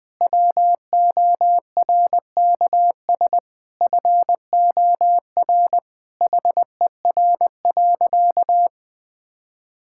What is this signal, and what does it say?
Morse code, heard as a keyed tone, about 21 wpm, 700 Hz. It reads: WORKS FOR HER.
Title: Sentences with 4-Word Phrases - 20wpm